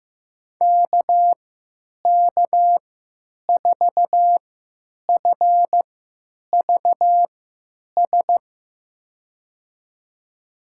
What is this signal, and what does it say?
Morse code, heard as a keyed tone, about 15 wpm, spaced as for 9 wpm, 700 Hz. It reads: KK4FVS
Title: USA Call Signs - 6 Characters - 3x Character Spacing - 15wpm